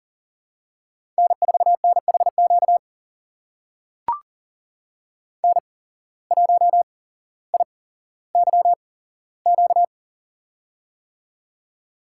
Morse code, heard as a keyed tone, about 42 wpm, spaced as for 17 wpm, 700 Hz. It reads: N4NHQ N1IYQ
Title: USA Call Signs - 5 Characters - 8x Character Spacing - 40wpm